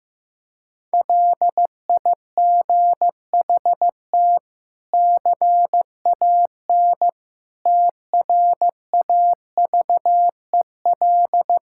Morse code, heard as a keyed tone, about 15 wpm, 700 Hz. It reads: LIGHT CAN TRAVEL
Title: Sentences from Top 1000 Words - Review Part 6 - 15wpm